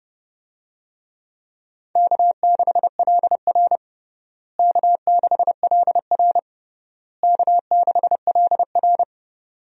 Morse code, heard as a keyed tone, about 31 wpm, 700 Hz. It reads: K6LR K6LR K6LR
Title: USA Call Signs - 4 Characters - Repeated 3x - 30wpm